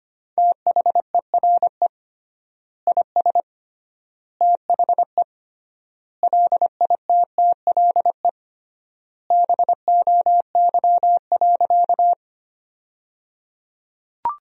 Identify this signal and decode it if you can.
Morse code, heard as a keyed tone, about 25 wpm, 700 Hz. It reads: THERE IS THE LITTLE BOY.
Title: 3x Word-Spacing - Sentences from Top 200 Words - 25wpm